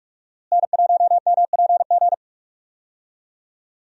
Morse code, heard as a keyed tone, about 46 wpm, 700 Hz. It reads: N1MPG
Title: USA Call Signs - 5 Characters - Repeated 2x - 45wpm